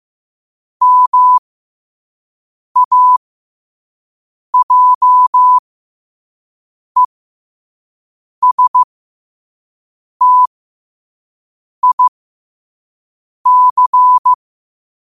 Morse code, heard as a keyed tone, about 15 wpm, spaced as for 5 wpm, 1000 Hz. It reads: MAJESTIC
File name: qotdfiles / 5wpm